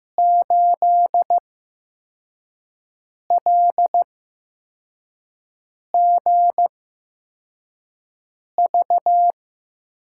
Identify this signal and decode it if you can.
Morse code, heard as a keyed tone, about 15 wpm, spaced as for 4 wpm, 700 Hz. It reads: 8LGV
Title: USA Call Signs - 6 Characters - 8x Character Spacing - 15wpm